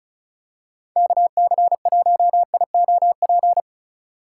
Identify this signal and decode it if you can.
Morse code, heard as a keyed tone, about 36 wpm, 700 Hz. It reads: KC1IOP
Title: USA Call Signs - 6 Characters - 6x Character Spacing - 35wpm